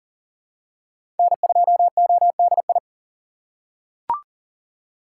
Morse code, heard as a keyed tone, about 37 wpm, 700 Hz. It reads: N2ODI
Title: USA Call Signs - 5 Characters - 6x Character Spacing - 40wpm